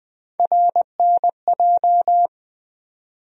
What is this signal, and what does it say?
Morse code, heard as a keyed tone, about 21 wpm, 700 Hz. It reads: RNJ
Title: USA Call Sign Postfixes - 3 Letters 20wpm